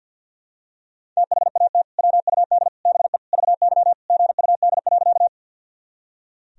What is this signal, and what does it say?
Morse code, heard as a keyed tone, about 46 wpm, 700 Hz. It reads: THAT WUD BE VY GUD.
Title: Sentences from Top 200 Words and Abbreviations 50wpm